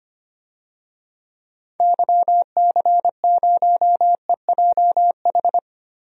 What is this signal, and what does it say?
Morse code, heard as a keyed tone, about 25 wpm, 700 Hz. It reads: YC0EJH
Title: DXCC-100 Call Signs - 6 Characters - Repeated 3x - 25wpm